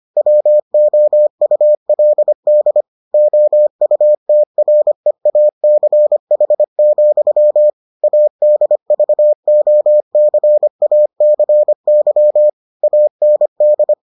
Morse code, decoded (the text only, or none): WOULD OUTREACH, ADVOCACY AND